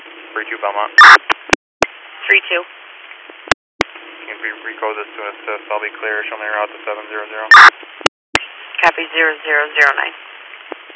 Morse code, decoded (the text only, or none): none